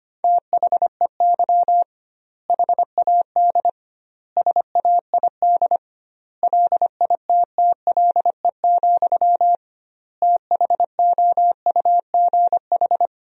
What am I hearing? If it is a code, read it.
Morse code, heard as a keyed tone, about 25 wpm, 700 Hz. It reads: THEY HAD SAID LITTLE, THOUGH
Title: 2x Word-Spacing - Sentences from Top 400 Words - 25wpm